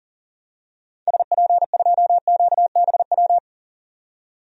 Morse code, heard as a keyed tone, about 40 wpm, 700 Hz. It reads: SP2QBW
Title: DXCC-100 Call Signs - 6 Characters - 40wpm